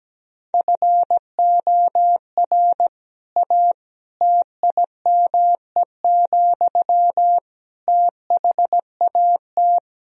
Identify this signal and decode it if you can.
Morse code, heard as a keyed tone, about 17 wpm, 700 Hz. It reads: FOR A TIME, THAT
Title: Sentences from Top 1000 Words - Review Part 4 - 17wpm